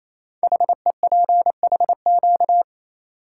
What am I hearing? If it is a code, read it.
Morse code, heard as a keyed tone, about 28 wpm, 700 Hz. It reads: HEPHQ
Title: Code Groups - 5 Letters - 28wpm